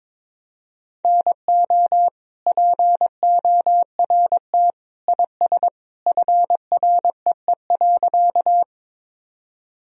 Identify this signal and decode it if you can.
Morse code, heard as a keyed tone, about 22 wpm, 700 Hz. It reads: NO PORT IS FREE.
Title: Sentences from Top 1000 Words - Review Part 1 - 22wpm